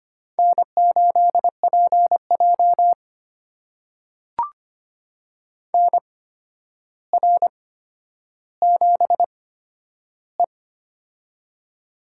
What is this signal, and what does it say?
Morse code, heard as a keyed tone, about 25 wpm, spaced as for 11 wpm, 700 Hz. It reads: N8PJ NR7E
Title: USA Call Signs - 4 Characters - 8x Character Spacing - 25wpm